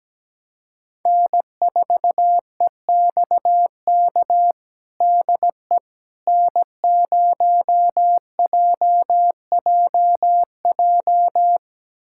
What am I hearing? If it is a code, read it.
Morse code, heard as a keyed tone, about 17 wpm, 700 Hz. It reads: N4EXK DE N0JJJ
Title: USA Callsign DE Callsign - 5 Characters - 17wpm